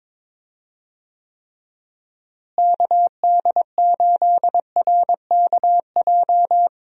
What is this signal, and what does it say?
Morse code, heard as a keyed tone, about 22 wpm, 700 Hz. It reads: KD8RKJ
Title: USA Call Signs - 6 Characters - Repeated 3x - 22wpm